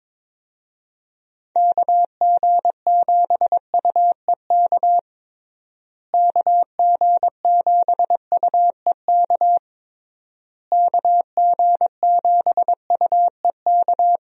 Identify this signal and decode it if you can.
Morse code, heard as a keyed tone, about 22 wpm, 700 Hz. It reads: KG7UEK KG7UEK KG7UEK
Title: USA Call Signs - 6 Characters - Repeated 3x - 22wpm